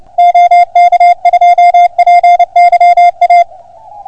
Morse code, heard as a keyed tone, about 30 wpm, 690 Hz. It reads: OK2PYA